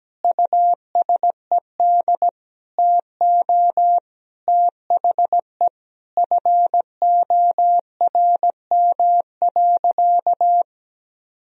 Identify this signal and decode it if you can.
Morse code, heard as a keyed tone, about 17 wpm, 700 Hz. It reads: USED TO THE FORM.
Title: Sentences from Top 1000 Words - Review Part 1 - 17wpm